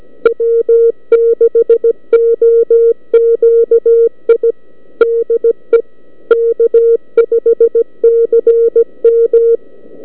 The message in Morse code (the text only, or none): W6OQI DE K5CM